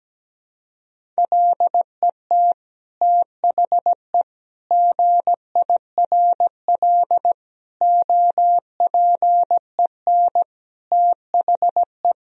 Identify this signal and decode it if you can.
Morse code, heard as a keyed tone, about 17 wpm, 700 Hz. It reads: LET THE GIRL OPEN THE
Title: Sentences from Top 1000 Words - Review Part 1 - 17wpm